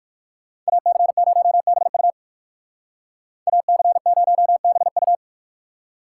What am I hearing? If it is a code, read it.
Morse code, heard as a keyed tone, about 59 wpm, 700 Hz. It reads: AC0BU AC0BU